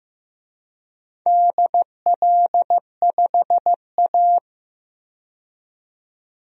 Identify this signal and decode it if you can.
Morse code, heard as a keyed tone, about 15 wpm, 700 Hz. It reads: DL5A
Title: DXCC-100 Call Signs - 4 Characters - 15wpm